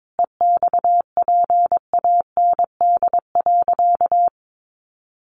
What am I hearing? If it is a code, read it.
Morse code, heard as a keyed tone, about 22 wpm, 700 Hz. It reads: EXPAND.
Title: Sentences from Top 1400 Words 22wpm